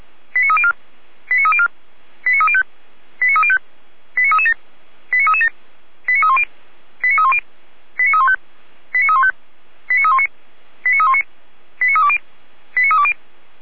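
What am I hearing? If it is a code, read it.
Five-tone selective call (ZVEI), heard as quick runs of five tones, tones of about 70 ms. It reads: 89304 89304 89305 89305 89337 89337 89310 89310 89315 89315 89319 89319 89320 89320
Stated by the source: Five-tone ZVEI selective call, received by SDR.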